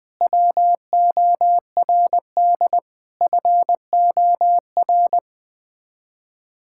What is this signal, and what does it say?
Morse code, heard as a keyed tone, about 20 wpm, 700 Hz. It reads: WORD FOR
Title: Sets of 8 Words - 20wpm